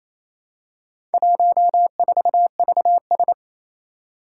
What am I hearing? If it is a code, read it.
Morse code, heard as a keyed tone, about 29 wpm, 700 Hz. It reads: 14VS
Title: Four Challenging Letter-Number Combinations 28wpm